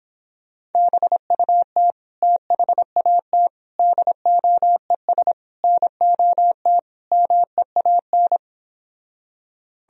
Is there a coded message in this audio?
Yes: Morse code from 0.7 to 8.4 s, heard as a keyed tone, about 29 wpm, 700 Hz. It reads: BUT THAT DOES NOT MEAN